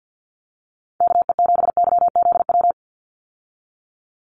Morse code, heard as a keyed tone, about 48 wpm, 700 Hz. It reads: KE7YZW
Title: USA Call Signs 50wpm